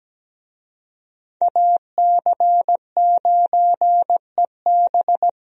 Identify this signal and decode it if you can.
Morse code, heard as a keyed tone, about 17 wpm, 700 Hz. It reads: AC9EB
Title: USA Call Signs - 5 Characters - Repeated 3x - 17wpm